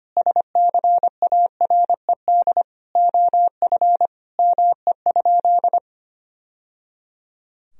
Morse code, heard as a keyed tone, about 25 wpm, 700 Hz. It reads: SCARED OF ME?
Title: Sentences from Top 1000 Words - Review Part 4 - 25wpm